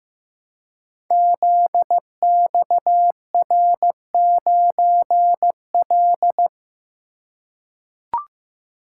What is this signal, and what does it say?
Morse code, heard as a keyed tone, about 15 wpm, 700 Hz. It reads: ZXR9L
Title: Code Groups - 5 Letter:Numbers - 15wpm